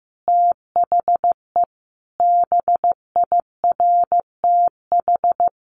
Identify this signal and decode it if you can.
Morse code, heard as a keyed tone, about 15 wpm, 700 Hz. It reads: THE BIRTH
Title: Sentences from Top 800 Words 15wpm